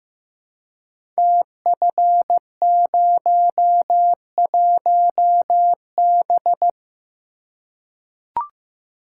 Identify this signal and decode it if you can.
Morse code, heard as a keyed tone, about 15 wpm, 700 Hz. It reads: TF01B
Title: Code Groups - 5 Letter:Numbers - 15wpm